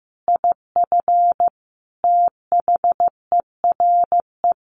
Morse code, heard as a keyed tone, about 15 wpm, 700 Hz. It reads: IF THERE
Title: Sentences from Top 1500 Words 15wpm